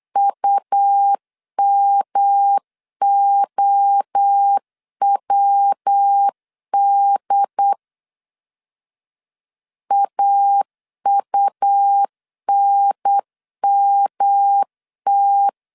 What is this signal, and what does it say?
Morse code, heard as a keyed tone, about 8 wpm, 800 Hz. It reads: UMOWD AUNMT
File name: CW / DET / 1 3 1500C S B 8wpm 4141